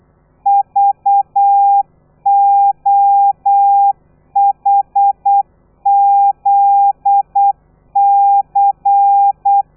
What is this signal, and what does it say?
Morse code, heard as a keyed tone, about 8 wpm, 800 Hz. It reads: VOHZC
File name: telegrafia / i2tfj / Lez13 35